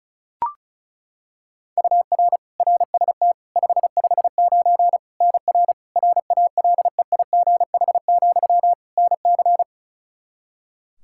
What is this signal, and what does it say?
Morse code, heard as a keyed tone, about 34 wpm, 700 Hz. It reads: UR RST 559 NR RALEIGH, NC
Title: Practice QSO - Exchange 1 - Beginning - 35wpm